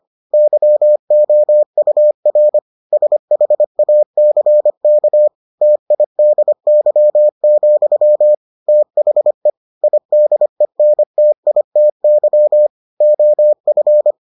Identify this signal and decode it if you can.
Morse code, heard as a keyed tone, about 25 wpm, 600 Hz. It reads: YOUR SHACK TIDY, THE IDENTITY OF